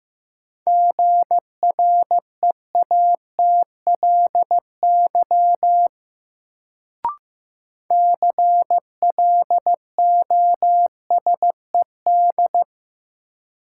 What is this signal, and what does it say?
Morse code, heard as a keyed tone, about 15 wpm, 700 Hz. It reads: GREATLY CLOSED
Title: Sentences from Top 1400 Words - New Word Drill Alt - 15wpm